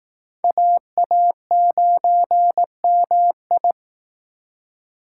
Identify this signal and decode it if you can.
Morse code, heard as a keyed tone, about 19 wpm, 700 Hz. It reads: AA9MI
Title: Speed Racing - USA 2x2 Call Signs 15wpm